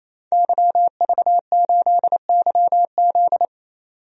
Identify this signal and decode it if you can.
Morse code, heard as a keyed tone, about 27 wpm, 700 Hz. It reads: YV8YZ
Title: DXCC-100 Call Signs - 5 Characters - 28wpm